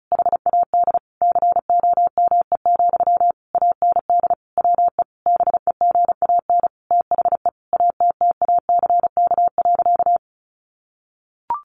Morse code, heard as a keyed tone, about 34 wpm, 700 Hz. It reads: HAD COME, AND WE BEGAN THE ATTACK.
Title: Sentences from Top 1500 Words 35wpm